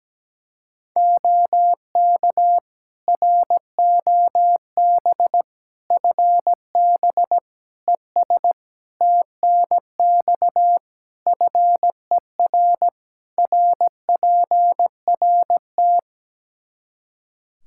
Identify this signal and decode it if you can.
Morse code, heard as a keyed tone, about 17 wpm, 700 Hz. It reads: OK ROB FB ES TNX FER RPRT
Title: Practice QSO - Exchange 2 - Info - 17wpm